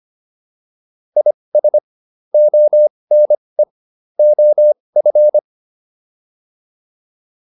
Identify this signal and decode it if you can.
Morse code, heard as a keyed tone, about 25 wpm, spaced as for 20 wpm, 600 Hz. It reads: IS ONE OF